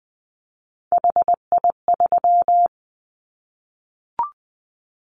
Morse code, heard as a keyed tone, about 20 wpm, 700 Hz. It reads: HI3
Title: Callsign Prefixes for Top 100 DXCC - 20wpm